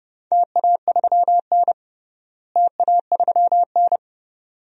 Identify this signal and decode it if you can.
Morse code, heard as a keyed tone, about 31 wpm, 700 Hz. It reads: TA3N TA3N
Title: DXCC-100 Call Signs - 4 Characters - Repeated 2x - 30wpm